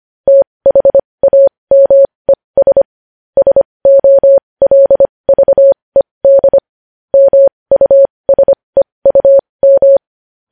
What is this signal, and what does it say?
Morse code, heard as a keyed tone, about 25 wpm, spaced as for 20 wpm, 550 Hz. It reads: THAMES SOLVED MUSEUM
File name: podcast / News headlines-1x-25x20